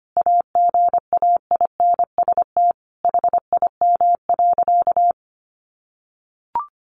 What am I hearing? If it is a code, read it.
Morse code, heard as a keyed tone, about 25 wpm, 700 Hz. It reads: AGAINST HIM.
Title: Sentences from Top 1300 Words 25wpm